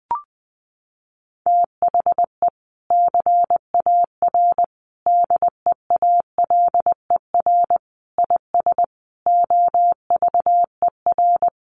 Morse code, heard as a keyed tone, about 20 wpm, 700 Hz. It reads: THE CAR DEALER IS OVER
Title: Sentences from Top 1100 Words 20wpm